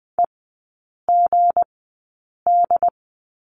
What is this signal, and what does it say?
Morse code, heard as a keyed tone, about 20 wpm, spaced as for 8 wpm, 700 Hz. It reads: EGD
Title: 178 Lesson 29e - Slash - Three Characters 20wpm